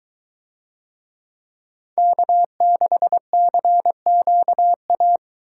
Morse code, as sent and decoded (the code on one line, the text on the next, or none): -.- -.... -.-. --.- .-
K6CQA